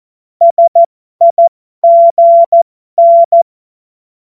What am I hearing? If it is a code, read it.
Morse code, heard as a keyed tone, about 14 wpm, spaced as for 12 wpm, 680 Hz. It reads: SIGN
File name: Morsekurs / Witze